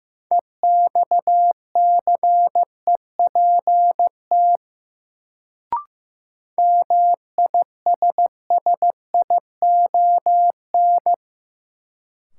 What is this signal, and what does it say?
Morse code, heard as a keyed tone, about 15 wpm, 700 Hz. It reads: EXCEPT MISSION
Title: Sentences from Top 1600 Words - New Word Drill - 15wpm